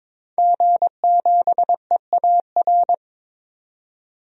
Morse code, heard as a keyed tone, about 22 wpm, 700 Hz. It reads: G7EAR
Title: DXCC-100 Call Signs - 5 Characters - 22wpm